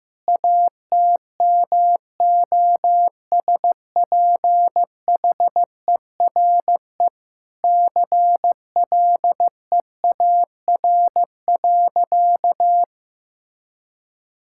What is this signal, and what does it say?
Morse code, heard as a keyed tone, about 15 wpm, 700 Hz. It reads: ATMOSPHERE CLEAR.
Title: Sentences from Top 1600 Words 15wpm